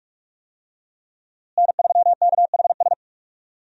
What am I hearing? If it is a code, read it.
Morse code, heard as a keyed tone, about 46 wpm, 700 Hz. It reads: N3KHS
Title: USA Call Signs - 5 Characters - Repeated 3x - 45wpm